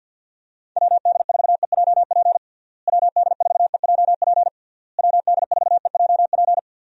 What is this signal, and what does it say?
Morse code, heard as a keyed tone, about 49 wpm, 700 Hz. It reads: WD4EJP WD4EJP WD4EJP
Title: USA Call Signs - 6 Characters - Repeated 3x - 50wpm